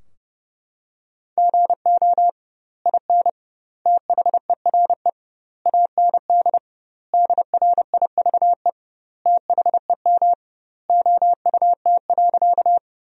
Morse code, heard as a keyed tone, about 30 wpm, 700 Hz. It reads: GO IN THERE AND DRIVE THEM OUT.
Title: 2x Word-Spacing - Sentences from Top 500 Words - 30wpm